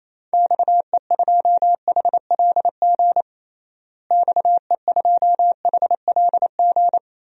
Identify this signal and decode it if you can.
Morse code, heard as a keyed tone, about 28 wpm, 700 Hz. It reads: XE2HLG XE2HLG
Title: DXCC-100 Call Signs - 6 Characters - Repeated 2x - 28wpm